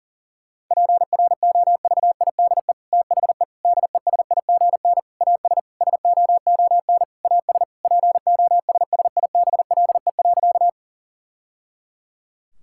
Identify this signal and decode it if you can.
Morse code, heard as a keyed tone, about 40 wpm, 700 Hz. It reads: PROVIDE THE DESIGN AS SOON AS POSSIBLE.
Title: Sentences from Top 1000 Words - Review Part 5 - 40wpm